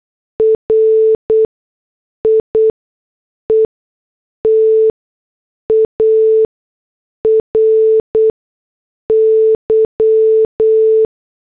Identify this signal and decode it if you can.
Morse code, heard as a keyed tone, about 8 wpm, spaced as for 6 wpm, 440 Hz. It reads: RIETARY